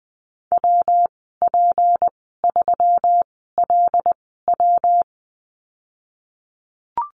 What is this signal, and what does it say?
Morse code, heard as a keyed tone, about 20 wpm, spaced as for 14 wpm, 700 Hz. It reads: WP3LW
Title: 154 Lesson 25g - Letter p - Call Signs 20wpm